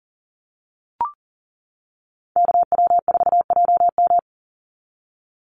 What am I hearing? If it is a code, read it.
Morse code, heard as a keyed tone, about 40 wpm, 700 Hz. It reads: KW4JM
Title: NA SOTA Chaser Callsigns 40wpm